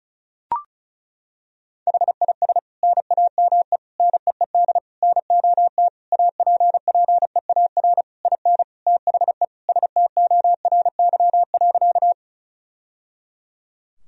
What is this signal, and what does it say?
Morse code, heard as a keyed tone, about 34 wpm, 700 Hz. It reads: HIS NAME NEED NOT APPEAR IN THE STORY.
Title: Sentences from Top 1000 Words - Review Part 2 - 35wpm